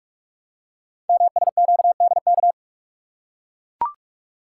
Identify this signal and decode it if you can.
Morse code, heard as a keyed tone, about 44 wpm, 700 Hz. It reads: MSQDK
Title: Code Groups - 5 Letters - 45wpm